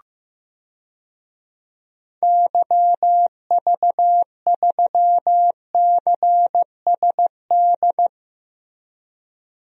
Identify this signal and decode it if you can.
Morse code, heard as a keyed tone, about 15 wpm, 700 Hz. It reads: YV3CSD
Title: DXCC-100 Call Signs - 6 Characters - Repeated 2x - 15wpm